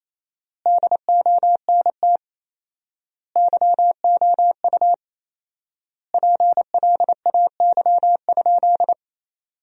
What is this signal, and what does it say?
Morse code, heard as a keyed tone, about 27 wpm, 700 Hz. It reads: DONT YOU PLAY?
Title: 4x Word-Spacing - Sentences from Top 300 Words - 28wpm